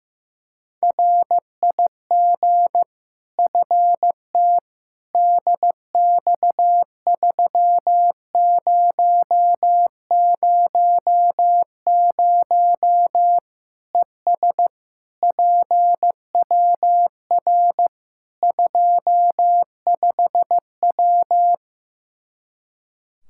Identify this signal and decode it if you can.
Morse code, heard as a keyed tone, about 15 wpm, 700 Hz. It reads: RIG FT DX3000 ES PWR 25W
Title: Practice QSO - Exchange 2 - Info - 15wpm